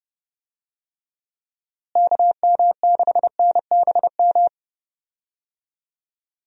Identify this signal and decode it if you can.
Morse code, heard as a keyed tone, about 31 wpm, 700 Hz. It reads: KM6NBM
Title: USA Call Signs - 6 Characters - Repeated 2x - 30wpm